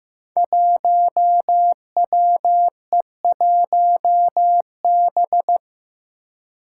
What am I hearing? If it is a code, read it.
Morse code, heard as a keyed tone, about 15 wpm, 700 Hz. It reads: 1WE1B
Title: Five Challenging Letter-Number Combinations 15wpm